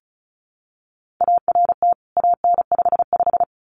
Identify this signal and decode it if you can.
Morse code, heard as a keyed tone, about 34 wpm, 700 Hz. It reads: ART AN55
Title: CWT Exchange 35wpm